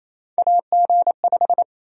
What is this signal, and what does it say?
Morse code, heard as a keyed tone, about 28 wpm, 700 Hz. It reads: AG5
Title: USA Call Sign Prefixes - 28wpm